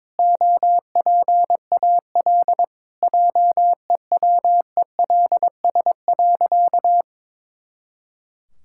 Morse code, heard as a keyed tone, about 22 wpm, 700 Hz. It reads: OPAL JEWELS.
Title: Pangram Challenge v2 - 22wpm